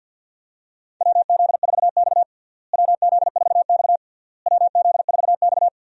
Morse code, heard as a keyed tone, about 50 wpm, 700 Hz. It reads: WZ4X WZ4X WZ4X
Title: USA Call Signs - 4 Characters - Repeated 3x - 50wpm